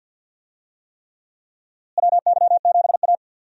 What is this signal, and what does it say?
Morse code, heard as a keyed tone, about 52 wpm, 700 Hz. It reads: WY7A
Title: USA Call Signs - 4 Characters - Repeated 3x - 50wpm